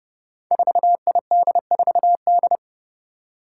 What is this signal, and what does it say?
Morse code, heard as a keyed tone, about 29 wpm, 700 Hz. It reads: 4ID4D
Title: Five Challenging Letter-Number Combinations 30wpm